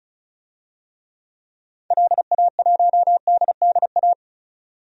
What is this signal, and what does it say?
Morse code, heard as a keyed tone, about 36 wpm, 690 Hz. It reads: LA1DDA
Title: DXCC-100 Call Signs - 6 Characters - 35wpm